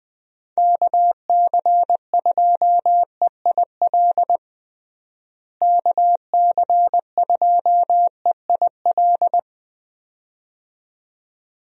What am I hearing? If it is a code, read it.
Morse code, heard as a keyed tone, about 20 wpm, 700 Hz. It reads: KC2EIL KC2EIL
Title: USA Call Signs - 6 Characters - Repeated 2x - 20wpm